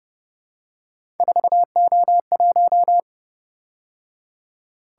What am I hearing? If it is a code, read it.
Morse code, heard as a keyed tone, about 29 wpm, 700 Hz. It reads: 4O1
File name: Speed-Racing - Three Letter-Number Combinations - 25wpm